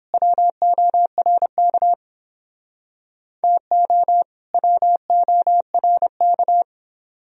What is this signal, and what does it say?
Morse code, heard as a keyed tone, about 30 wpm, 700 Hz. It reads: WORK TO WORK